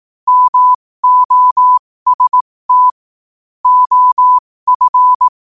Morse code, heard as a keyed tone, about 18 wpm, spaced as for 15 wpm, 1000 Hz. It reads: MOST OF
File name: qotdfiles / 15wpm